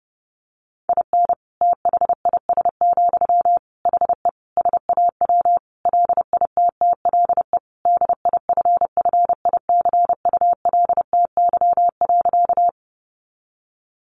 Morse code, heard as a keyed tone, about 30 wpm, 700 Hz. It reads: IN THIS, HE SAW LITTLE DIFFICULTY.
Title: Sentences from Top 1500 Words 30wpm